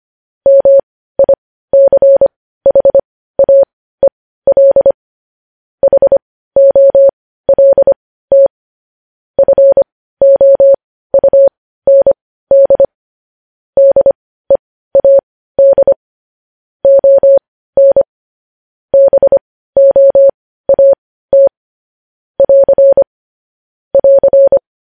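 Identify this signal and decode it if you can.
Morse code, heard as a keyed tone, about 25 wpm, spaced as for 15 wpm, 550 Hz. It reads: MICHAEL HOLT FOUND DEAD ON BOAT <AR> <AR>
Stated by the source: Morse code news headlines at 15 WPM.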